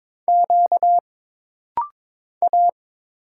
Morse code, heard as a keyed tone, about 22 wpm, 700 Hz. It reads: Q A
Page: This rapid-fire timing is geared to push your ICR (Instant Character Recognition) skills.